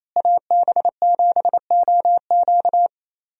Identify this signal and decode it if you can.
Morse code, heard as a keyed tone, about 28 wpm, 700 Hz. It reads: AB7OQ
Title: USA Call Signs - 5 Characters - Repeated 3x - 28wpm